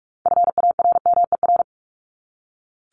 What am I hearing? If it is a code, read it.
Morse code, heard as a keyed tone, about 44 wpm, 700 Hz. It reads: FARMER
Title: Top 1751-2000 Words 45wpm